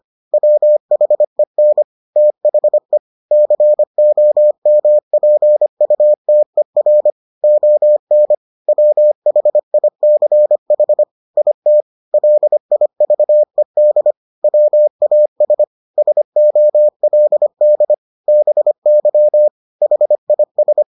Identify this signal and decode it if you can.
Morse code, heard as a keyed tone, about 25 wpm, 600 Hz. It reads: WHEN THE COMPUTER ON WHICH IT LIVED WAS SOLD BY HIS